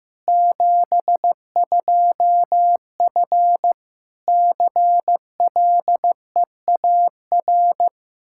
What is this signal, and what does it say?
Morse code, heard as a keyed tone, about 15 wpm, 700 Hz. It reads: 72F CLEAR
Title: Practice QSO - Exchange 2 - Info - 15wpm